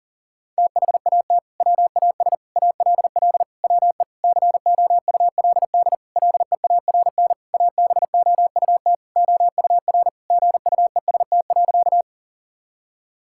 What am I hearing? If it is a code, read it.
Morse code, heard as a keyed tone, about 40 wpm, 700 Hz. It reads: THAT WAS ALL WE COULD LEARN ABOUT OUR GUEST.
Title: Sentences from Top 1600 Words 40wpm